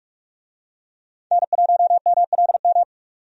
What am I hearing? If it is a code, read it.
Morse code, heard as a keyed tone, about 43 wpm, 700 Hz. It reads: N1MLM
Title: USA Call Signs - 5 Characters - 45wpm